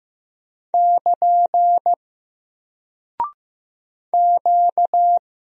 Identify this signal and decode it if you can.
Morse code, heard as a keyed tone, about 15 wpm, 700 Hz. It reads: <KN> Q
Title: Single Character (Letter:Number:Punctuation:Prosign) 15wpm